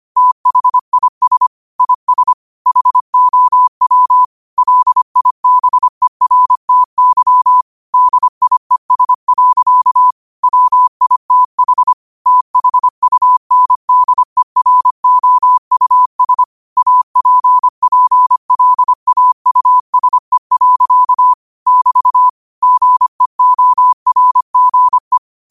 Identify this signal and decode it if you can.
Morse code, heard as a keyed tone, about 25 wpm, 1000 Hz. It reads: THIS IS HOW LIBERTY DIES. WITH THUNDEROUS APPLAUSE. <BT> GEORGE
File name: qotdfiles / 25wpm